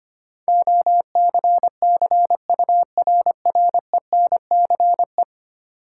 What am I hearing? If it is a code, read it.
Morse code, heard as a keyed tone, about 25 wpm, 690 Hz. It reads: OCCURRENCE
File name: 10 Letter Words - First 3 Letters Given - 25wpm